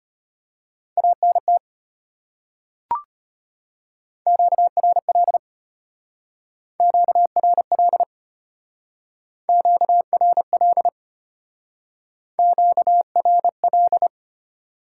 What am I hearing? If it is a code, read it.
Morse code, heard as a keyed tone, about 33 wpm, 690 Hz. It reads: ANT QRL QRL QRL QRL